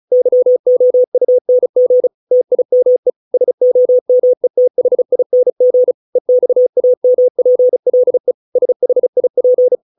Morse code, heard as a keyed tone, about 34 wpm, 500 Hz. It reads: YOUNG TIME SOMETHING EXAMPLE SHIP